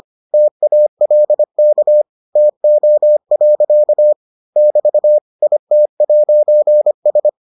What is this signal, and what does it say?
Morse code, heard as a keyed tone, about 25 wpm, 600 Hz. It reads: TALK TO. <BT> IT'S